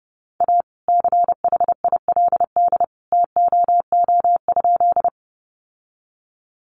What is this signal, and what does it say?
Morse code, heard as a keyed tone, about 30 wpm, 700 Hz. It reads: A CHILD TOO?
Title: Sentences from Top 800 Words 30wpm